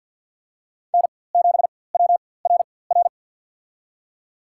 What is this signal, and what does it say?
Morse code, heard as a keyed tone, about 53 wpm, spaced as for 24 wpm, 700 Hz. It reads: N7WRR
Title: USA Call Signs - 5 Characters - 4x Character Spacing - 50wpm